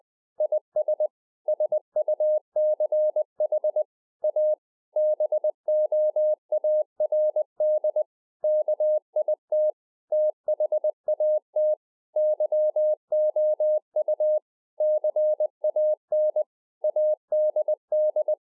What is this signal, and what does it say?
Morse code, heard as a keyed tone, about 20 wpm, 610 Hz. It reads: IS SUCH A BOARD KIT THAT YOU CAN ADD
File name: 2011-04-mp3 / A SMALL TRANCEIVER 3 20WPM